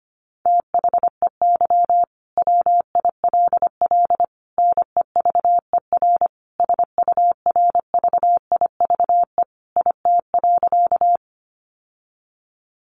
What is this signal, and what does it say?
Morse code, heard as a keyed tone, about 25 wpm, 700 Hz. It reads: THEY WILL NEVER SURVIVE IT.
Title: Sentences from Top 1300 Words 25wpm